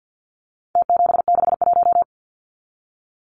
Morse code, heard as a keyed tone, about 50 wpm, 700 Hz. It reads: T761
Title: Numbers - Contest Style - 4-Digit 50wpm